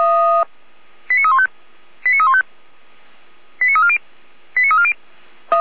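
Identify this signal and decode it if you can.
Five-tone selective call (ZVEI), heard as quick runs of five tones, tones of about 70 ms. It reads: 89315 89315 89340 89340
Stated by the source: ZVEI five-tone selective call received on an SDR, logged by Feuerwehr Orlamünde fire brigade.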